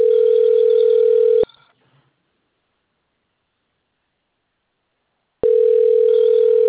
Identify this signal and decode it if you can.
Ringback tone (North America), 440 + 480 Hz.
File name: rg-1600-unknown- - - .11495